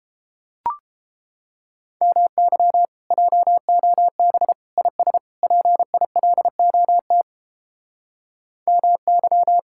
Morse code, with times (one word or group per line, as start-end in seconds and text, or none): 2.0-2.9 MY
3.1-4.5 JOB
4.8-5.2 IS
5.4-7.2 PILOT
8.7-9.6 MY